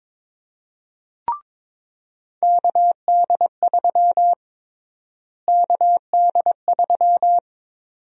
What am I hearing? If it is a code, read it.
Morse code, heard as a keyed tone, about 22 wpm, 700 Hz. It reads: KD3 KD3
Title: USA Call Sign Prefixes - Repeated 2x - 22wpm